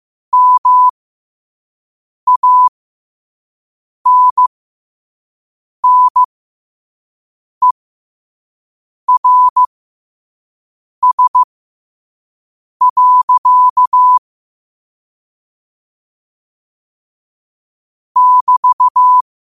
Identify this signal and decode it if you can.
Morse code, heard as a keyed tone, about 15 wpm, spaced as for 5 wpm, 1000 Hz. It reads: MANNERS. <BT>
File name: qotdfiles / 5wpm